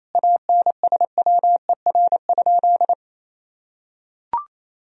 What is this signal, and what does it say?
Morse code, heard as a keyed tone, about 28 wpm, 700 Hz. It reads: ANSWER?
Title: Sentences from Top 1000 Words - Review Part 2 - 28wpm